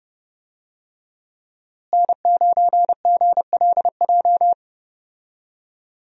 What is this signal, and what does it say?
Morse code, heard as a keyed tone, about 30 wpm, 700 Hz. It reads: N9GLJ
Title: USA Call Signs - 5 Characters - Repeated 3x - 30wpm